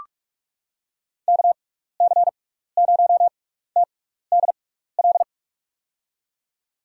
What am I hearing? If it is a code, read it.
Morse code, heard as a keyed tone, about 44 wpm, spaced as for 15 wpm, 700 Hz. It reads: KC0TDL
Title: USA Call Signs - 6 Characters - 6x Character Spacing - 45wpm